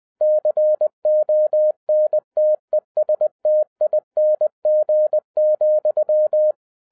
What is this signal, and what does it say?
Morse code, heard as a keyed tone, about 20 wpm, 610 Hz. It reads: CONTESTING,
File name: 2011-10-mp3 / n13-TIPS FROM MY 34 YEARS OF EXPERIENCE 20wpm